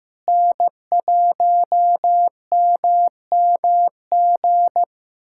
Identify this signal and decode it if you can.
Morse code, heard as a keyed tone, about 15 wpm, 700 Hz. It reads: N1MMG
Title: USA Callsign DE Callsign - 5 Characters - 15wpm